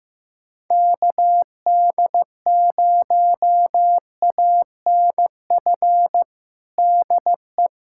Morse code, heard as a keyed tone, about 15 wpm, 700 Hz. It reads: KD0ANF DE
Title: USA Callsign DE Callsign - 6 Characters - 15wpm